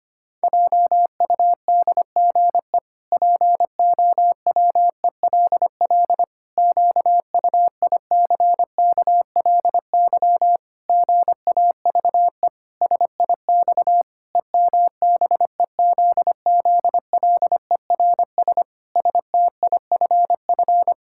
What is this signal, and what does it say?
Morse code, heard as a keyed tone, about 25 wpm, 700 Hz. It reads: JUDGE POWELL QUICKLY GAVE SIX EMBEZZLERS STIFF
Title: Pangram Challenge v2 - 25wpm